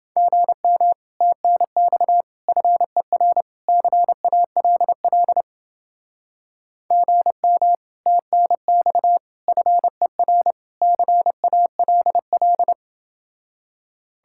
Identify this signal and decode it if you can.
Morse code, heard as a keyed tone, about 29 wpm, 700 Hz. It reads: GM TNX FER CALL GM TNX FER CALL